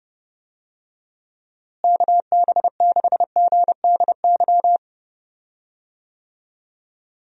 Morse code, heard as a keyed tone, about 30 wpm, 700 Hz. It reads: KB6GDY
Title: USA Call Signs - 6 Characters - Repeated 3x - 30wpm